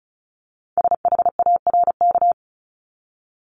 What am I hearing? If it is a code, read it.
Morse code, heard as a keyed tone, about 36 wpm, 700 Hz. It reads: SHARK
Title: Multi-letter Phonemes 35wpm